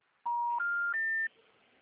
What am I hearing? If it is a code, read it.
Special information tone.
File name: out- -2011- - - .22